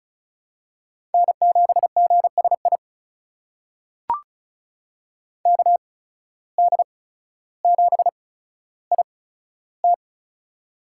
Morse code, heard as a keyed tone, about 34 wpm, spaced as for 15 wpm, 700 Hz. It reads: N7GSI KD7IT
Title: USA Call Signs - 5 Characters - 8x Character Spacing - 35wpm